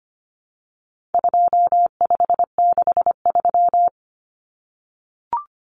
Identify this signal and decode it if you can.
Morse code, heard as a keyed tone, about 25 wpm, 700 Hz. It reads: 2563
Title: Common Cut Numbers - 4-Digit 25wpm